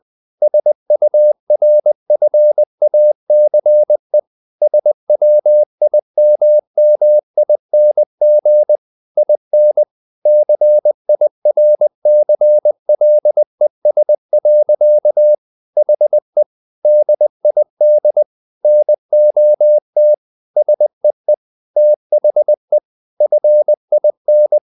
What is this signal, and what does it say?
Morse code, heard as a keyed tone, about 20 wpm, 600 Hz. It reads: SURFACE SWIMMING IN CIRCLES. HE DID NOT SEE THE FIN
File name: the old man and the sea part 2 20wpm